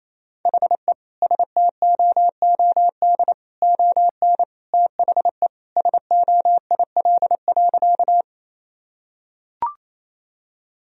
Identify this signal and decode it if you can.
Morse code, heard as a keyed tone, about 28 wpm, 700 Hz. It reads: HE STOOD ON THE SOIL.
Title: Sentences from Top 1000 Words - Review Part 4 - 28wpm